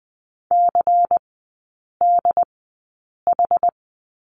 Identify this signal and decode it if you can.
Morse code, heard as a keyed tone, about 20 wpm, spaced as for 8 wpm, 700 Hz. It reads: CDH
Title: 100 Lesson 17e - Letter u - Three Characters 20wpm